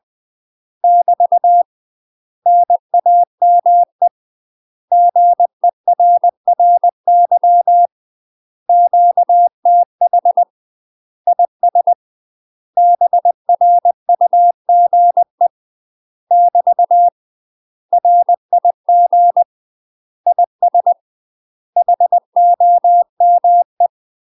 Morse code, heard as a keyed tone, about 20 wpm, 700 Hz. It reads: <BT> NAME GERRY QTH IS BRUGE <BT> RIG IS HOME
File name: UK MASTER QSOs TRAINING / 20WPM / UK MASTER QSOs 0064